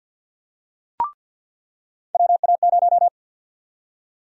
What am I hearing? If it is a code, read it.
Morse code, heard as a keyed tone, about 46 wpm, 700 Hz. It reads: WA0
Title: USA Call Sign Prefixes - 50wpm